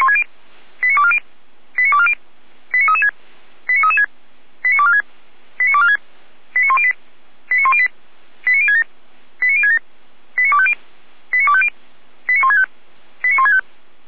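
Five-tone selective call (ZVEI), heard as quick runs of five tones, tones of about 70 ms. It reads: 89240 89240 89336 89336 89246 89246 89108 89108 89067 89067 89250 89250 89165 89165
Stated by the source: Five-tone ZVEI selective call 89075, received by SDR.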